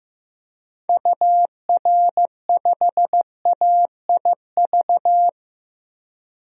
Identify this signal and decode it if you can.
Morse code, heard as a keyed tone, about 15 wpm, 700 Hz. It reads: UR5AIV
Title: DXCC-100 Call Signs - 6 Characters - 15wpm